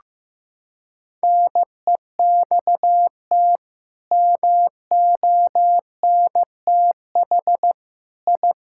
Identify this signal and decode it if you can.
Morse code, heard as a keyed tone, about 15 wpm, 700 Hz. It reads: NEXT MONTH I
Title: Sentences from Top 1000 Words - Review Part 5 - 15wpm